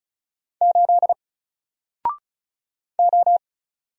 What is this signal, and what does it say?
Morse code, heard as a keyed tone, about 33 wpm, 700 Hz. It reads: 8 O
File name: Single-Letter-Number-35wpm